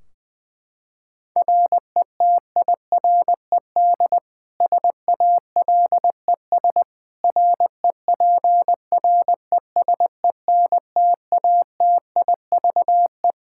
Morse code, heard as a keyed tone, about 20 wpm, 700 Hz. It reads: RETIRED SALES REPRESENTATIVE
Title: Practice QSO - Exchange 3 - More Info - 20wpm